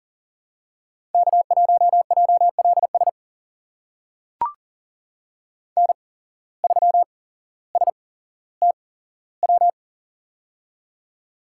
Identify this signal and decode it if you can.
Morse code, heard as a keyed tone, about 39 wpm, spaced as for 17 wpm, 700 Hz. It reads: K1JLS N3STW
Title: USA Call Signs - 5 Characters - 8x Character Spacing - 40wpm